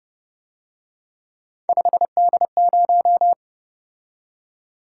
Morse code, heard as a keyed tone, about 29 wpm, 700 Hz. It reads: 5D0